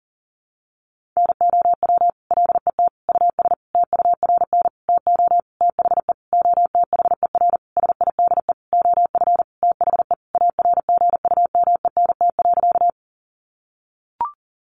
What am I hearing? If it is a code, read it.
Morse code, heard as a keyed tone, about 40 wpm, 700 Hz. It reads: NOW LET US TURN TO THE OTHER SIDE OF THE ARGUMENT.
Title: Sentences from Top 1400 Words 40wpm